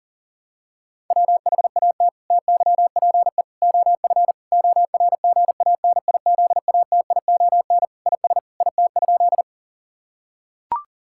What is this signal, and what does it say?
Morse code, heard as a keyed tone, about 40 wpm, 700 Hz. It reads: WHAT TYPE OF ORGANIZATION IS IT?
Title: Sentences from Top 1000 Words - Review Part 5 - 40wpm